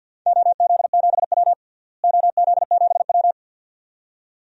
Morse code, heard as a keyed tone, about 52 wpm, 700 Hz. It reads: OZ7W OZ7W
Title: DXCC-100 Call Signs - 4 Characters - Repeated 2x - 50wpm